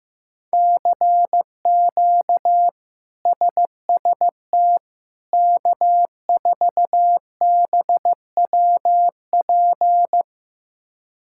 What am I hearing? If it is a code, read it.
Morse code, heard as a keyed tone, about 15 wpm, 700 Hz. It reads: CQ SST K4BWP
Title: K1USN SST QSO Practice - 15wpm